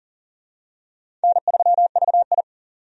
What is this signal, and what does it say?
Morse code, heard as a keyed tone, about 40 wpm, 700 Hz. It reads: N3VI
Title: USA Call Signs - 4 Characters - 40wpm